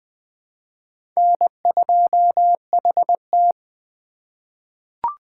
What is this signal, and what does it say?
Morse code, heard as a keyed tone, about 20 wpm, 700 Hz. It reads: N2HT
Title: USA Call Signs - 4x Character Spacing - 20wpm